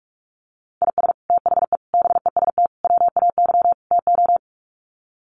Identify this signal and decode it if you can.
Morse code, heard as a keyed tone, about 44 wpm, 700 Hz. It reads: IS THE BEST WAY TO
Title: Sets of 5 Words Encore 45wpm